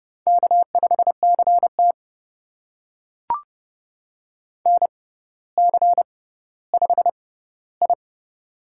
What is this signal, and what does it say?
Morse code, heard as a keyed tone, about 30 wpm, spaced as for 15 wpm, 690 Hz. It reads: K5CT NC5I
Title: USA Call Signs - 4 Characters - 6x Character Spacing - 30wpm